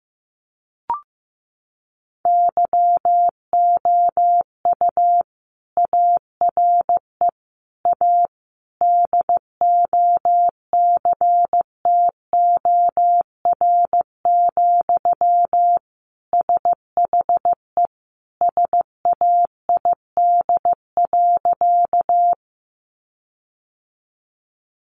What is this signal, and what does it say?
Morse code, heard as a keyed tone, about 15 wpm, 700 Hz. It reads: YOU ARE A DOCTOR, SHE SAID.
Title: Sentences from Top 1500 Words 15wpm